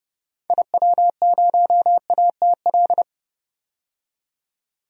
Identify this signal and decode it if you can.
Morse code, heard as a keyed tone, about 30 wpm, 700 Hz. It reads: IW0ATL
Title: DXCC-100 Call Signs - 6 Characters - 30wpm